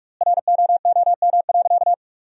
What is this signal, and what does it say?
Morse code, heard as a keyed tone, about 41 wpm, 700 Hz. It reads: ROOM.
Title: 4x Word-Spacing - Sentences from Top 500 Words - 45wpm